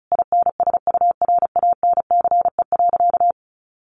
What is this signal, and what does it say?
Morse code, heard as a keyed tone, about 34 wpm, 700 Hz. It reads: INSURANCE.
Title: A Sound Of Thunder 35wpm